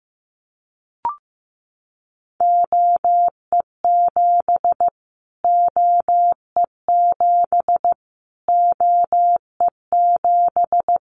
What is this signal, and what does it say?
Morse code, heard as a keyed tone, about 15 wpm, 700 Hz. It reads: OE7 OE7 OE7
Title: Callsign Prefixes for Top 25 DXCC - Repeated 3x - 15wpm